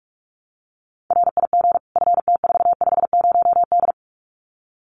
Morse code, heard as a keyed tone, about 44 wpm, 700 Hz. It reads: RIG FT450D
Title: Common QSO Phrases 45wpm